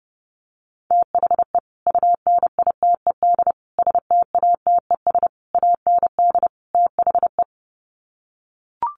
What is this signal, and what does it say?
Morse code, heard as a keyed tone, about 30 wpm, 700 Hz. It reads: THE UNITED STATES AND THE
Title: Sets of 5 Words 30wpm